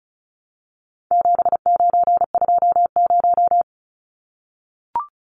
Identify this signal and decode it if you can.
Morse code, heard as a keyed tone, about 34 wpm, 700 Hz. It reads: 7920
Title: Numbers - 4-Digit 35wpm